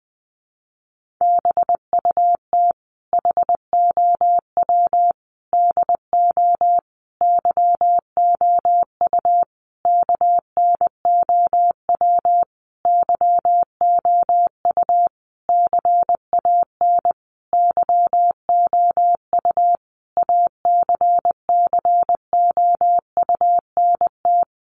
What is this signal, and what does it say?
Morse code, heard as a keyed tone, about 20 wpm, 700 Hz. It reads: BUT HOW DO YOU KNOW YOU CAN YOU ACCOUNT
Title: Sentences from Top 900 Words 20wpm